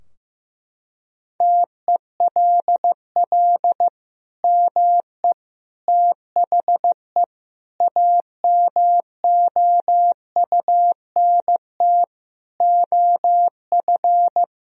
Morse code, heard as a keyed tone, about 15 wpm, 700 Hz. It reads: TELL ME THE AMOUNT OF
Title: Sentences from Top 1000 Words - Review Part 4 - 15wpm